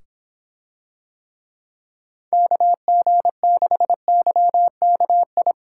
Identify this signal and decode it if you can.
Morse code, heard as a keyed tone, about 24 wpm, 700 Hz. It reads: KG6YKI